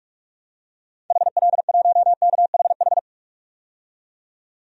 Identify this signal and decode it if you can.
Morse code, heard as a keyed tone, about 46 wpm, 700 Hz. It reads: HL1KHH
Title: DXCC-100 Call Signs - 6 Characters - 45wpm